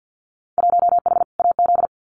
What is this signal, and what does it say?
Morse code, heard as a keyed tone, about 50 wpm, 700 Hz. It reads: 1H AZ
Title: Winter Field Day Exchanges 50wpm